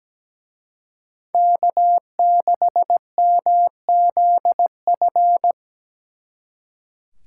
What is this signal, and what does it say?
Morse code, heard as a keyed tone, about 20 wpm, 700 Hz. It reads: K6MZF
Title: Speed Racing - USA 1x3 Call Signs 17wpm